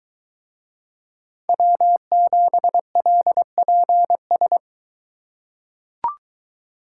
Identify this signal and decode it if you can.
Morse code, heard as a keyed tone, about 22 wpm, 700 Hz. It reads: W7LPS